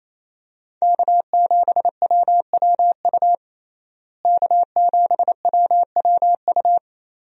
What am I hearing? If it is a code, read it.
Morse code, heard as a keyed tone, about 28 wpm, 700 Hz. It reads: K7WWU K7WWU
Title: USA Call Signs - 5 Characters - Repeated 2x - 28wpm